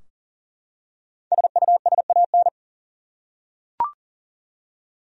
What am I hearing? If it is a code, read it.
Morse code, heard as a keyed tone, about 40 wpm, 700 Hz. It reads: SUSAN
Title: USA Common Names - Max 5 Letters - 40wpm